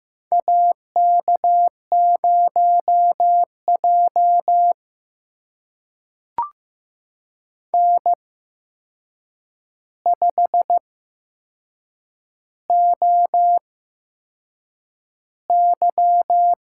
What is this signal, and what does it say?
Morse code, heard as a keyed tone, about 15 wpm, spaced as for 6 wpm, 700 Hz. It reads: AK0J N5OY
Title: USA Call Signs - 4 Characters - 8x Character Spacing - 15wpm